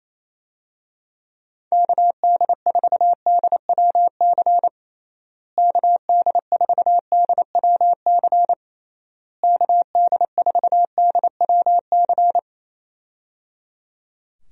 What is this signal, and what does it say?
Morse code, heard as a keyed tone, about 28 wpm, 700 Hz. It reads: KD4DWC KD4DWC KD4DWC
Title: USA Call Signs - 6 Characters - Repeated 3x - 28wpm